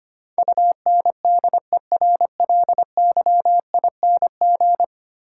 Morse code, heard as a keyed tone, about 25 wpm, 700 Hz. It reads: UNDERLYING
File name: 10 Letter Words - First 3 Letters Given - 25wpm